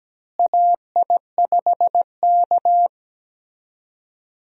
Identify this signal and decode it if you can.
Morse code, heard as a keyed tone, about 17 wpm, 700 Hz. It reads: AI5K
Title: USA Call Signs - 4 Characters - Repeated 2x - 17wpm